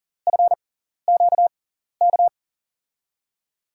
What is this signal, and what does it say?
Morse code, heard as a keyed tone, about 40 wpm, spaced as for 14 wpm, 700 Hz. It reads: FQK
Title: USA Call Signs - 5 Characters - 6x Character Spacing - 40wpm